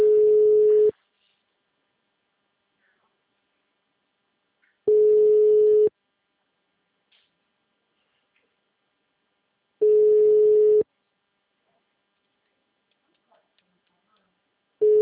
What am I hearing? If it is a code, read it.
Ringback tone (Europe, 425 Hz).
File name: out- -2012- - - .372